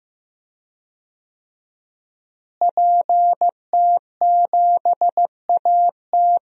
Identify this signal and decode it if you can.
Morse code, heard as a keyed tone, about 15 wpm, 700 Hz. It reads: PT7AT
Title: DXCC-100 Call Signs - 5 Characters - Repeated 2x - 15wpm